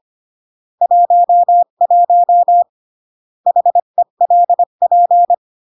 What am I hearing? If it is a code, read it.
Morse code, heard as a keyed tone, about 25 wpm, 700 Hz. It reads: 11 HELP
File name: CW / jodorowsky / 82 commandments 25-20